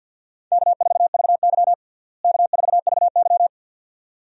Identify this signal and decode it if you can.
Morse code, heard as a keyed tone, about 49 wpm, 700 Hz. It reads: K4VY K4VY
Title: USA Call Signs - 4 Characters - Repeated 2x - 50wpm